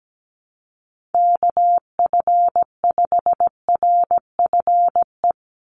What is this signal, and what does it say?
Morse code, heard as a keyed tone, about 17 wpm, 700 Hz. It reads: KF5RFE
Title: USA Call Signs 17wpm